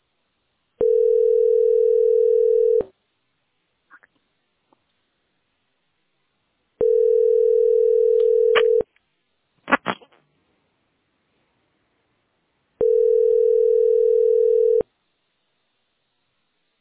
Ringback tone (North America), 440 + 480 Hz.